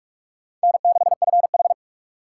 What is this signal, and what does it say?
Morse code, heard as a keyed tone, about 42 wpm, 700 Hz. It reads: N6FH
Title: USA Call Signs - 4 Characters - 2x Character Spacing - 45wpm